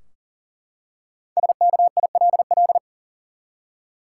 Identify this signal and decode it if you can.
Morse code, heard as a keyed tone, about 37 wpm, 700 Hz. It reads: SKILL
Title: Five Letter Words - 40wpm